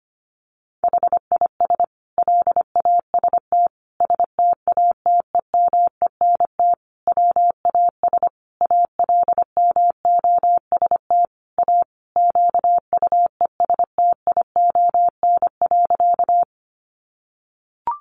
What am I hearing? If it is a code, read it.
Morse code, heard as a keyed tone, about 25 wpm, 700 Hz. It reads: HIS LAST STATEMENT WAS ALMOST A QUESTION.
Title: Sentences from Top 1200 Words 25wpm